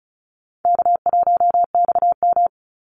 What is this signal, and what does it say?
Morse code, heard as a keyed tone, about 34 wpm, 700 Hz. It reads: K1XM
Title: Call Signs Repeated 2x - 35wpm